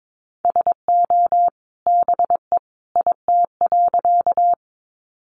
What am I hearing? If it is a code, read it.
Morse code, heard as a keyed tone, about 22 wpm, 700 Hz. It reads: SO BE IT.
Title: Sentences from Top 100 Words 22wpm